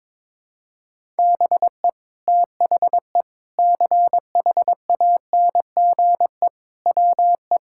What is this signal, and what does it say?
Morse code, heard as a keyed tone, about 22 wpm, 700 Hz. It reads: BE THE CHANGE WE
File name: Sentences with 9-Word Phrases - 22wpm